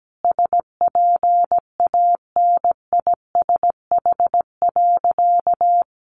Morse code, heard as a keyed tone, about 17 wpm, 700 Hz. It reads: SPANISH.
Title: Additional Easy Phrases 17wpm